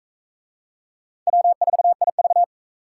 Morse code, heard as a keyed tone, about 35 wpm, 700 Hz. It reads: W4IV